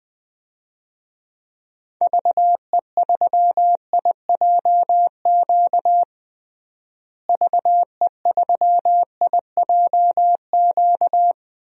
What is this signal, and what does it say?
Morse code, heard as a keyed tone, about 21 wpm, 700 Hz. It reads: VE3IJQ VE3IJQ
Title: DXCC-100 Call Signs - 6 Characters - Repeated 2x - 20wpm